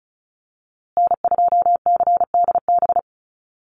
Morse code, heard as a keyed tone, about 34 wpm, 700 Hz. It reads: N2CDB
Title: USA Call Signs Repeated 2x 35wpm